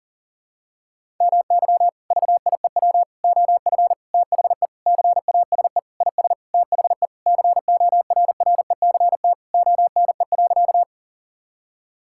Morse code, heard as a keyed tone, about 40 wpm, 700 Hz. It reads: MY VIEW OF THE CASE IS THE CORRECT ONE.
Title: Sentences from Top 1000 Words - Review Part 6 - 40wpm